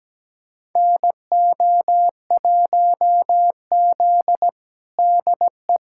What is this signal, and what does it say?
Morse code, heard as a keyed tone, about 17 wpm, 700 Hz. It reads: NO1Z DE
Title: USA Callsign DE Callsign - 4 Characters - 17wpm